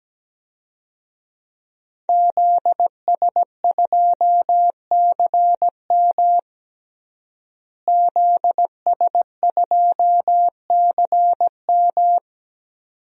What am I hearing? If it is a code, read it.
Morse code, heard as a keyed tone, about 17 wpm, 700 Hz. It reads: ZS2CM ZS2CM
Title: DXCC-100 Call Signs - 5 Characters - Repeated 2x - 17wpm